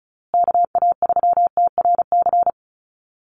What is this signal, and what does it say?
Morse code, heard as a keyed tone, about 34 wpm, 700 Hz. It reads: KA3TRC
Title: USA Call Signs Repeated 3x 35wpm